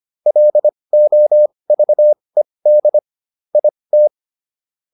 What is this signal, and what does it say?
Morse code, heard as a keyed tone, about 25 wpm, spaced as for 20 wpm, 600 Hz. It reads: LOVED IT